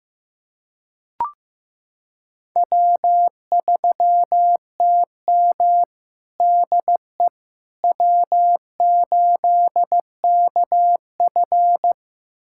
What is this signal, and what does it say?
Morse code, heard as a keyed tone, about 15 wpm, 700 Hz. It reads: W3TM DE W8KF
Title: USA Callsign DE Callsign - 4 Characters - 15wpm